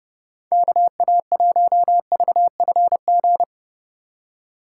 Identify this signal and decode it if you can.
Morse code, heard as a keyed tone, about 30 wpm, 700 Hz. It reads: KA1VFG
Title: USA Call Signs - 6 Characters - 4x Character Spacing - 30wpm